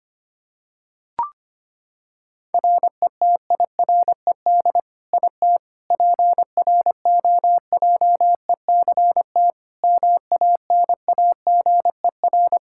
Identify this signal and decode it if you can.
Morse code, heard as a keyed tone, about 25 wpm, 700 Hz. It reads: RETIRED IT PROJECT MANAGER
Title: Practice QSO - Exchange 3 - More Info - 25wpm